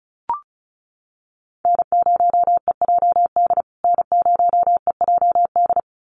Morse code, heard as a keyed tone, about 34 wpm, 700 Hz. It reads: N0EJD N0EJD
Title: USA Call Signs Repeated 2x 35wpm